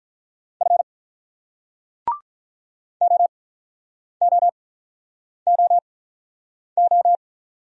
Morse code, heard as a keyed tone, about 42 wpm, spaced as for 9 wpm, 700 Hz. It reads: F OOOO
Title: Speed-Racing Single Letters 35wpm